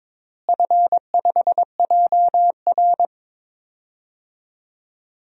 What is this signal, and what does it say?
Morse code, heard as a keyed tone, about 21 wpm, 700 Hz. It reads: F5JR
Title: DXCC-100 Call Signs - 4 Characters - Repeated 3x - 22wpm